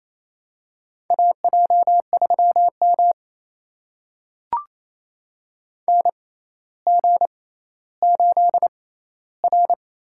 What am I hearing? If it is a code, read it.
Morse code, heard as a keyed tone, about 28 wpm, spaced as for 14 wpm, 700 Hz. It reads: AJ3M NG8R
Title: USA Call Signs - 6x Character Spacing - 28wpm